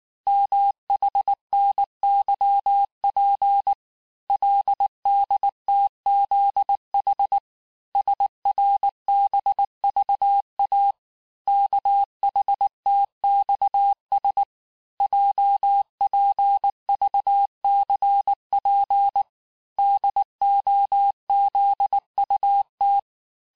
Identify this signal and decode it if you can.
Morse code, heard as a keyed tone, about 19 wpm, 800 Hz. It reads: MHNYP LDTZH SRBVA KHTXS JPVCP DOZUT
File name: cours-du-lundi files / cours-1595-teste-CHIFFRE-vitesse18-MPM